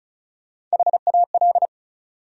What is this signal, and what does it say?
Morse code, heard as a keyed tone, about 35 wpm, 700 Hz. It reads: HAL
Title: USA Common Names - Max 5 Letters - 35wpm